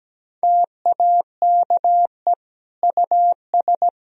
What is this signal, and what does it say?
Morse code, heard as a keyed tone, about 17 wpm, 700 Hz. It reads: TAKE US
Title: Sentences from Top 1000 Words - Review Part 1 - 17wpm